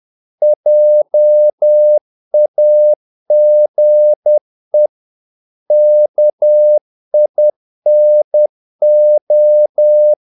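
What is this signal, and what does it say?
Morse code, heard as a keyed tone, about 10 wpm, 600 Hz. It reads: JAGE KINO
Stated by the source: Morse practice of short Danish words at 10 wpm, from CW med Gnister.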